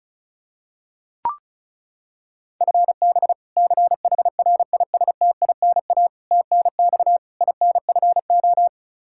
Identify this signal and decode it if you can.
Morse code, heard as a keyed tone, about 34 wpm, 700 Hz. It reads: FB CHRISTINA TNX INFO
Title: Practice QSO - Exchange 3 - More Info - 35wpm